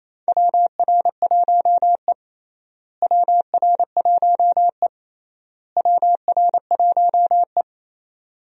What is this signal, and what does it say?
Morse code, heard as a keyed tone, about 28 wpm, 700 Hz. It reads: WR1E WR1E WR1E
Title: USA Call Signs - 4 Characters - Repeated 3x - 28wpm